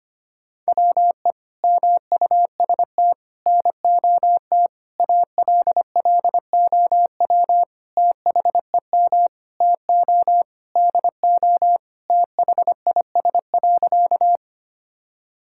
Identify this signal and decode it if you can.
Morse code, heard as a keyed tone, about 25 wpm, 700 Hz. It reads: WE MUST NOT ALLOW THEM TO DO THIS.
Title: Sentences from Top 1000 Words - Review Part 5 - 25wpm